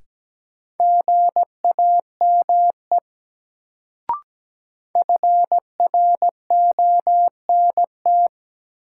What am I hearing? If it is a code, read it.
Morse code, heard as a keyed tone, about 17 wpm, 700 Hz. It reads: GAME FRONT
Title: Sentences from Top 500 Words - New Word Drill - 17wpm